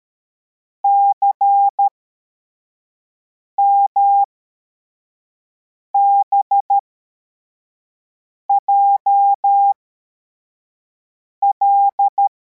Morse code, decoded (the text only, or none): CMBJL